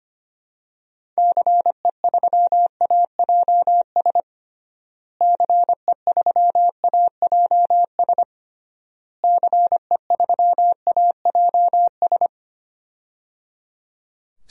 Morse code, heard as a keyed tone, about 25 wpm, 700 Hz. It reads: CE3AJS CE3AJS CE3AJS
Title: DXCC-100 Call Signs - 6 Characters - Repeated 3x - 25wpm